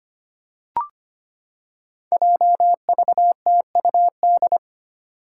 Morse code, heard as a keyed tone, about 25 wpm, 700 Hz. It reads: JVTUD
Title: Five Challenging Letter-Number Combinations 25wpm